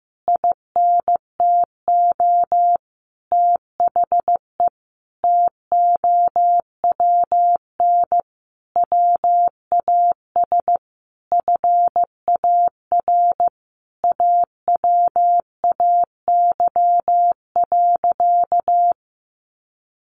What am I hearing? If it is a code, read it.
Morse code, heard as a keyed tone, about 15 wpm, 700 Hz. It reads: INTO THE TOWN WAS FAR AWAY.
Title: Sentences from Top 1500 Words 15wpm